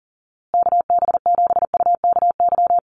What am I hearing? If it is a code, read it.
Morse code, heard as a keyed tone, about 37 wpm, 700 Hz. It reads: KB7UKY
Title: USA Call Signs Repeated 2x 40wpm